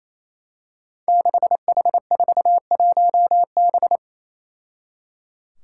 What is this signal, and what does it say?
Morse code, heard as a keyed tone, about 29 wpm, 700 Hz. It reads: 6H41B
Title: Five Challenging Letter-Number Combinations 28wpm